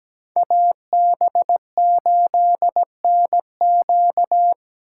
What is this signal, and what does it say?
Morse code, heard as a keyed tone, about 20 wpm, 700 Hz. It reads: AB8NQ
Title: Speed Racing - USA 2x2 Call Signs 17wpm